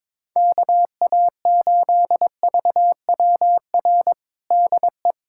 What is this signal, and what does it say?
Morse code, heard as a keyed tone, about 22 wpm, 700 Hz. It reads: KA8VWR DE
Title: USA Callsign DE Callsign - 6 Characters - 22wpm